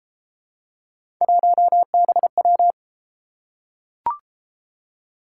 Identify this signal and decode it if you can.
Morse code, heard as a keyed tone, about 31 wpm, 700 Hz. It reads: 1BW